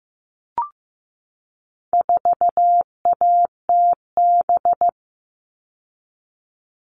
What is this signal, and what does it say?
Morse code, heard as a keyed tone, about 15 wpm, 700 Hz. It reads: 4ATB
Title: All Cut Numbers - 4-Digit 15wpm